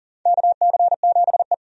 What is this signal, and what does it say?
Morse code, heard as a keyed tone, about 40 wpm, 700 Hz. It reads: KC7E
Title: USA Call Signs - 4 Characters - 40wpm